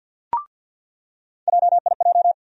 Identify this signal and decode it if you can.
Morse code, heard as a keyed tone, about 50 wpm, 700 Hz. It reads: JIJ
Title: USA Call Sign Postfixes - 3 Letters 50wpm